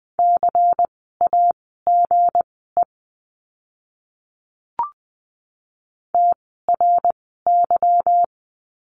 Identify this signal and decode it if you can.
Morse code, heard as a keyed tone, about 20 wpm, spaced as for 15 wpm, 700 Hz. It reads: CAGE TRY
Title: 160 Lesson 26f - Letter g - Words 20wpm